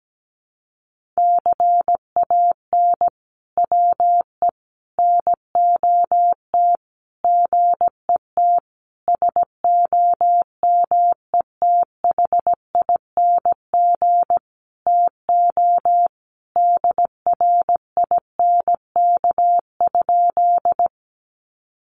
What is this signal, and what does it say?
Morse code, heard as a keyed tone, about 17 wpm, 700 Hz. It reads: CAN WE NOT GET SOMETHING TO DRINK?
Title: Sentences from Top 1500 Words 17wpm